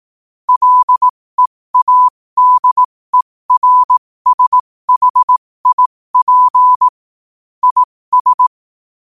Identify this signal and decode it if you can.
Morse code, heard as a keyed tone, about 18 wpm, spaced as for 15 wpm, 1000 Hz. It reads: LEADERSHIP IS
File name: qotdfiles / 15wpm